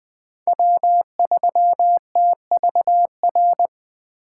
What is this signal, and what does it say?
Morse code, heard as a keyed tone, about 20 wpm, 700 Hz. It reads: W3TVR
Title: USA Call Signs - 5 Characters - 8x Character Spacing - 20wpm